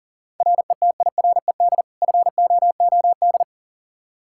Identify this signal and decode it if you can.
Morse code, heard as a keyed tone, about 40 wpm, 700 Hz. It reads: RETIRED FOOD
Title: Practice QSO - Exchange 3 - More Info - 40wpm